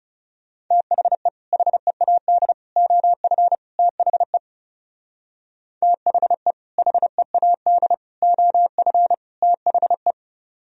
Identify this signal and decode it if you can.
Morse code, heard as a keyed tone, about 37 wpm, 700 Hz. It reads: THE HEAD OF THE THE HEAD OF THE
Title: Speed-Racing - Sets of 4 Words 30wpm